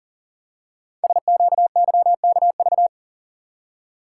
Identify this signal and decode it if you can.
Morse code, heard as a keyed tone, about 40 wpm, 700 Hz. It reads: SQYKV
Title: Code Groups - 5 Letters - 40wpm